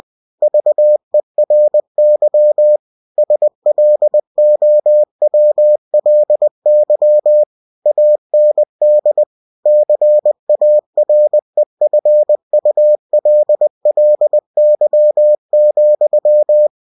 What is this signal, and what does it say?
Morse code, heard as a keyed tone, about 20 wpm, 600 Hz. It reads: VERY SLOWLY AND CAREFULLY,